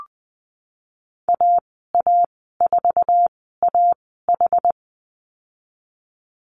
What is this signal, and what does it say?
Morse code, heard as a keyed tone, about 19 wpm, spaced as for 14 wpm, 700 Hz. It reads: AA4AH
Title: 076 Lesson 13g - Letter L - Call Signs 20wpm